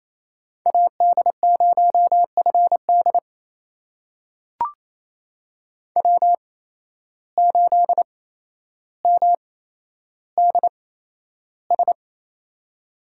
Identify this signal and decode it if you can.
Morse code, heard as a keyed tone, about 28 wpm, spaced as for 12 wpm, 700 Hz. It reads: AD0FD W8MDS
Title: USA Call Signs - 5 Characters - 8x Character Spacing - 28wpm